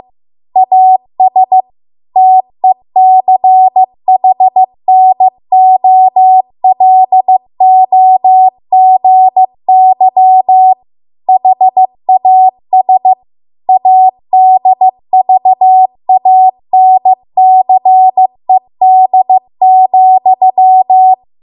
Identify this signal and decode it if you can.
Morse code, heard as a keyed tone, about 15 wpm, 750 Hz. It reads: AS TECHNOLOGY HAS ADVANCED,
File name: Archive / 15 WPM / 15WPM